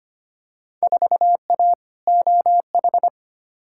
Morse code, heard as a keyed tone, about 25 wpm, 700 Hz. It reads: 4A OH
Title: Field Day Exchanges - 25wpm